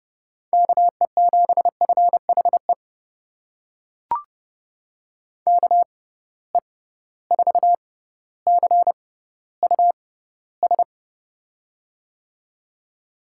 Morse code, heard as a keyed tone, about 30 wpm, spaced as for 15 wpm, 700 Hz. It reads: KE7FHE KE4CUS
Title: USA Call Signs - 6 Characters - 6x Character Spacing - 30wpm